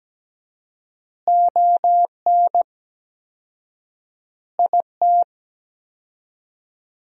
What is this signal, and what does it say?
Morse code, heard as a keyed tone, about 17 wpm, 700 Hz. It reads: ON IT
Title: 4x Word-Spacing - Sentences from Top 400 Words - 17wpm